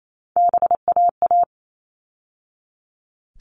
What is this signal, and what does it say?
Morse code, heard as a keyed tone, about 28 wpm, 700 Hz. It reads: BAA
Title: All Cut Numbers - 3-Digit 28wpm